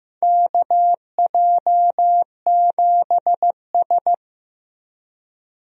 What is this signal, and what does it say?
Morse code, heard as a keyed tone, about 15 wpm, 700 Hz. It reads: KJ7S
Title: USA Call Signs - 4 Characters - Repeated 2x - 15wpm